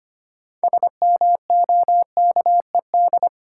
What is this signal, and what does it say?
Morse code, heard as a keyed tone, about 25 wpm, 690 Hz. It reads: SMOKED
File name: practice / 25wpm / Difficult Words with Hints - 25wpm